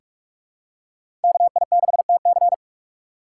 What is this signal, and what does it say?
Morse code, heard as a keyed tone, about 43 wpm, 700 Hz. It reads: KI6TC
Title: USA Call Signs - 5 Characters - 6x Character Spacing - 45wpm